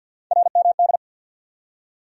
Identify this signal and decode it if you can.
Morse code, heard as a keyed tone, about 53 wpm, 700 Hz. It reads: RMD
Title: USA Call Sign Postfixes - 3 Letters 50wpm